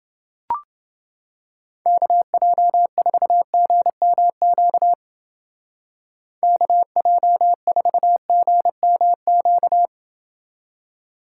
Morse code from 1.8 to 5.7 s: KJ4GMQ

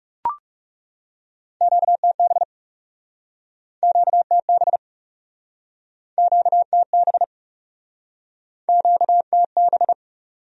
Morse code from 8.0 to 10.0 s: QTB

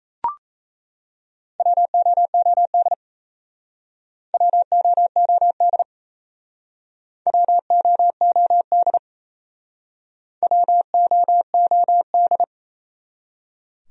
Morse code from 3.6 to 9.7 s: WOOD WOOD

